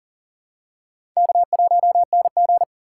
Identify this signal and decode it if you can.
Morse code, heard as a keyed tone, about 40 wpm, 700 Hz. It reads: K1NG
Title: USA Call Signs - 4 Characters - 6x Character Spacing - 40wpm